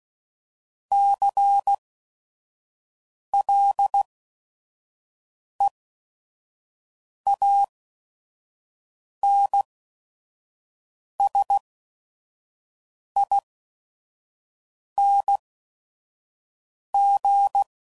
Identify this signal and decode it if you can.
Morse code, heard as a keyed tone, about 16 wpm, spaced as for 5 wpm, 800 Hz. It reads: CLEANSING
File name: Suffixes / Special ING's / VerySlowToFaster ING's